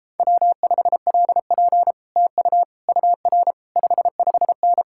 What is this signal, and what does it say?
Morse code, heard as a keyed tone, about 31 wpm, 700 Hz. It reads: W5LP TU UR 55N